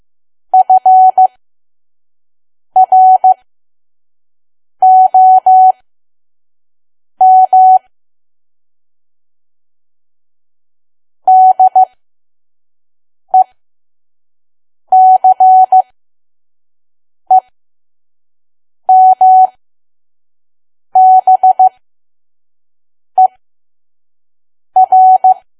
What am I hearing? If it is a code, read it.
Morse code, heard as a keyed tone, about 15 wpm, spaced as for 5 wpm, 750 Hz. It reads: FROM DECEMBER
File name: Archive / 5 WPM files / PM